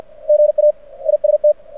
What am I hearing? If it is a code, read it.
Morse code, heard as a keyed tone, about 52 wpm, 600 Hz. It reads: GA IST
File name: archived / CRSnet